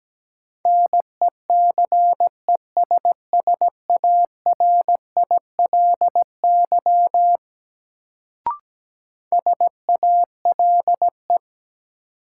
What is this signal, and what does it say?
Morse code, heard as a keyed tone, about 17 wpm, 700 Hz. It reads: NECESSARILY SALE
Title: Sentences from Top 1400 Words - New Word Drill - 17wpm